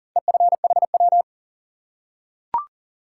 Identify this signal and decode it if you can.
Morse code, heard as a keyed tone, about 42 wpm, 700 Hz. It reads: EFHW
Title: Top 180 QSO Elements 40wpm